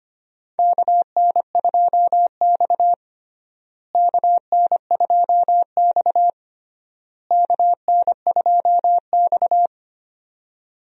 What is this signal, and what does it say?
Morse code, heard as a keyed tone, about 25 wpm, 700 Hz. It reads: KN2X KN2X KN2X
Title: USA Call Signs - 4 Characters - Repeated 3x - 25wpm